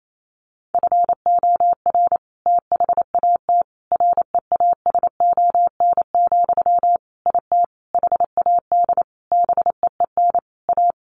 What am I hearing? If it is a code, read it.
Morse code, heard as a keyed tone, about 28 wpm, 700 Hz. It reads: FOR THAT REASON, IT HAD BEEN A
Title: Sentences from Top 1500 Words 28wpm